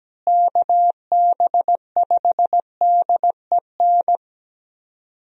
Morse code, heard as a keyed tone, about 17 wpm, 700 Hz. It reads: KB5DEN
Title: USA Call Signs - 6 Characters - Repeated 2x - 17wpm